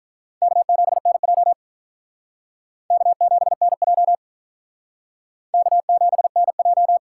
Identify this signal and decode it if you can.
Morse code, heard as a keyed tone, about 45 wpm, 700 Hz. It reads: K7NJ K7NJ K7NJ